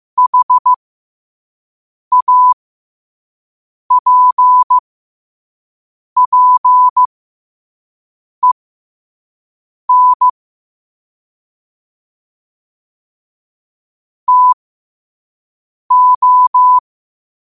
Morse code, heard as a keyed tone, about 15 wpm, spaced as for 5 wpm, 1000 Hz. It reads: HAPPEN TO
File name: qotdfiles / 5wpm